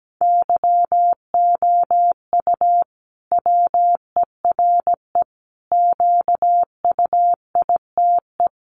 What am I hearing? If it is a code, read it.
Morse code, heard as a keyed tone, about 17 wpm, 700 Hz. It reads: YOU WERE QUITE
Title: Sentences from Top 1200 Words 17wpm